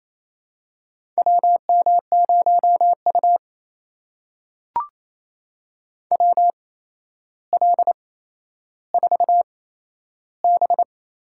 Morse code, heard as a keyed tone, about 28 wpm, spaced as for 12 wpm, 700 Hz. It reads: WM0U WL4B
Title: USA Call Signs - 4 Characters - 8x Character Spacing - 28wpm